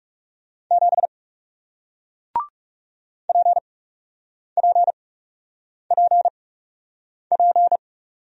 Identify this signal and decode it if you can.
Morse code, heard as a keyed tone, about 39 wpm, spaced as for 8 wpm, 700 Hz. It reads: 7 PPPP